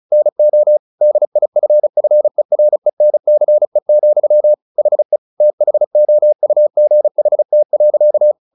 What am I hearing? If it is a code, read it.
Morse code, heard as a keyed tone, about 34 wpm, 600 Hz. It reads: NO DIFFERENCE, HE THOUGHT.
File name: the old man and the sea / the old man and the sea 1 35wpm